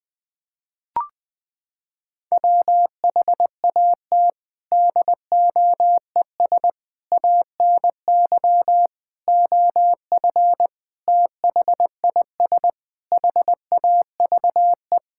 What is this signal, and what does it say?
Morse code, heard as a keyed tone, about 21 wpm, 700 Hz. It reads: WHAT DOES ANY OF THIS HAVE
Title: Sets of 9 Words - 20wpm